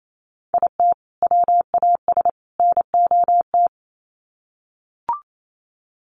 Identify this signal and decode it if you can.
Morse code, heard as a keyed tone, about 28 wpm, 700 Hz. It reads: IT WAS NOT
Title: Sets of 3 Words 28wpm